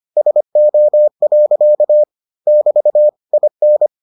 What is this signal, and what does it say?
Morse code, heard as a keyed tone, about 25 wpm, 600 Hz. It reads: SO. <BT> IN